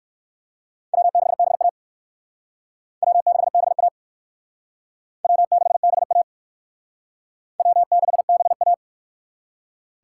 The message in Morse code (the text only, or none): W6BA W6BA W6BA W6BA